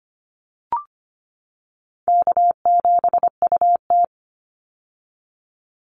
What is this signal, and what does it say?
Morse code, heard as a keyed tone, about 25 wpm, 700 Hz. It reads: K7UT
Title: Easy Call Signs - 25wpm